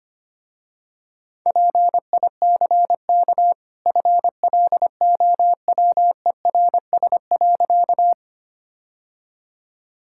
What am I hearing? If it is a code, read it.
Morse code, heard as a keyed tone, about 25 wpm, 700 Hz. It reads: PICK FLOWERS.
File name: Easy Sentences with Hints - 25wpm